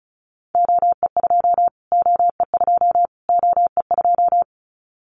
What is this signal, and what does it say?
Morse code, heard as a keyed tone, about 34 wpm, 700 Hz. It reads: OE2 OE2 OE2
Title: Callsign Prefixes for Top 25 DXCC - Repeated 3x - 35wpm